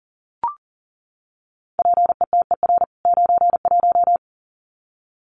Morse code, heard as a keyed tone, about 40 wpm, 700 Hz. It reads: PETER 91
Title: CWT Exchange 40wpm